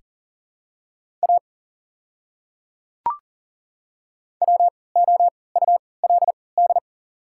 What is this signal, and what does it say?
Morse code, heard as a keyed tone, about 40 wpm, spaced as for 23 wpm, 700 Hz. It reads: A WOULD
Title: 3x Character Spacing - Top 100 Words - 40wpm